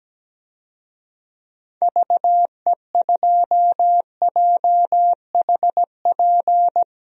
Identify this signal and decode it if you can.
Morse code, heard as a keyed tone, about 17 wpm, 700 Hz. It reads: VE2JHP
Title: DXCC-100 Call Signs - 6 Characters - 17wpm